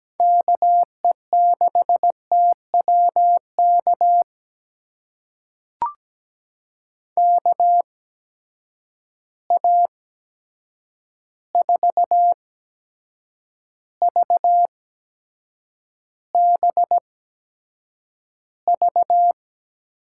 Morse code, heard as a keyed tone, about 17 wpm, spaced as for 7 wpm, 700 Hz. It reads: KE6TWK KA4VBV
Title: USA Call Signs - 8x Character Spacing - 17wpm